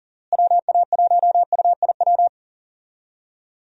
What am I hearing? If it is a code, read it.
Morse code, heard as a keyed tone, about 40 wpm, 700 Hz. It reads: WA1UIW
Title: USA Call Signs - 6 Characters - Repeated 3x - 40wpm